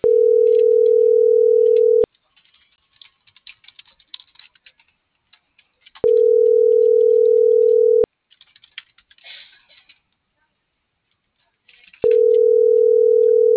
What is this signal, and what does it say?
Ringback tone (North America), 440 + 480 Hz.